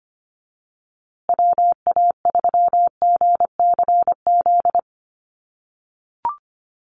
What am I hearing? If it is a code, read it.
Morse code, heard as a keyed tone, about 25 wpm, 700 Hz. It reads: WA3GCZ
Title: USA Call Signs Repeated 3x 25wpm